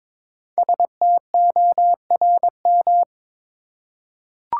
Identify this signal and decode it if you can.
Morse code, heard as a keyed tone, about 22 wpm, 700 Hz. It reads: STORM
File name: 5 Letter Words - First 3 Letters Given - 22wpm